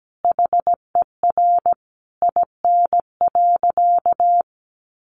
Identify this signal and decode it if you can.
Morse code, heard as a keyed tone, about 17 wpm, 700 Hz. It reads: HER IN.
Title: Sentences from Top 1500 Words 17wpm